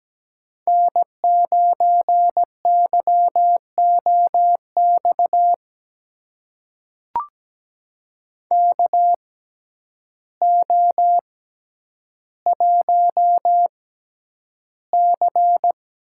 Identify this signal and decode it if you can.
Morse code, heard as a keyed tone, about 17 wpm, spaced as for 9 wpm, 700 Hz. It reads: N9YOX KO1C
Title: USA Call Signs - 6x Character Spacing - 17wpm